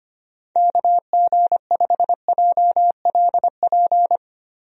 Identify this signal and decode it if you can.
Morse code, heard as a keyed tone, about 25 wpm, 700 Hz. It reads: KG5JLP
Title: USA Call Signs - 6 Characters - 25wpm